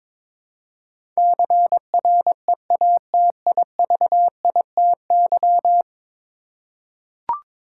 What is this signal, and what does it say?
Morse code, heard as a keyed tone, about 22 wpm, 700 Hz. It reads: CREATIVITY
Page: morse-code-ninja-practice-22wpm